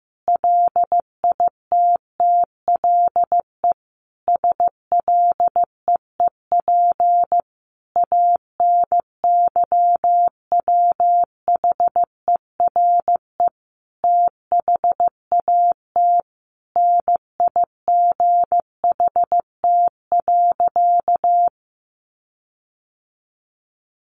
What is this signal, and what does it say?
Morse code, heard as a keyed tone, about 15 wpm, 700 Hz. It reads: LITTLE SLEEP ANYWHERE THAT NIGHT.
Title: Sentences from Top 1400 Words 15wpm